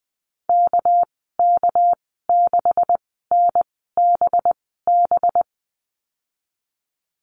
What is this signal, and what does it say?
Morse code, heard as a keyed tone, about 20 wpm, spaced as for 15 wpm, 700 Hz. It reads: KK6NBB
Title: 213 Lesson 34g - Number 8 - Call Signs 20wpm